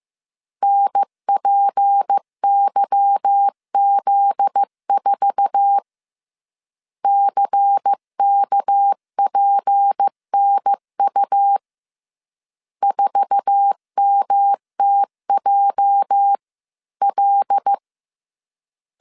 Morse code, heard as a keyed tone, about 15 wpm, 800 Hz. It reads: NPYZ4 CKPNU 4MTJL
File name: CW / DET / 1 9 1500C S A 14wpm 3222